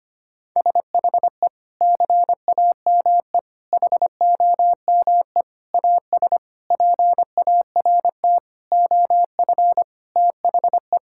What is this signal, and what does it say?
Morse code, heard as a keyed tone, about 25 wpm, 700 Hz. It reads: SHE CAME HOME AS PART OF THE
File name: practice / 25wpm / Sentences with 4-Word Phrases - 25wpm